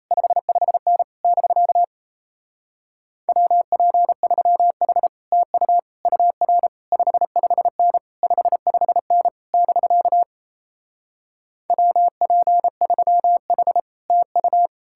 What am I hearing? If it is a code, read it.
Morse code, heard as a keyed tone, about 38 wpm, 700 Hz. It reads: 55N <BK> WP3H TU UR 55N 55N <BK> WP3H TU